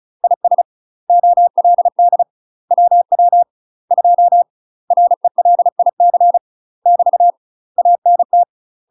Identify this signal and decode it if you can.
Morse code, heard as a keyed tone, about 34 wpm, 700 Hz. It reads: IS OLD WW 2 RELIC <BT> ANT
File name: UK MASTER QSOs TRAINING / 35WPM / UK MASTER QSOs 0030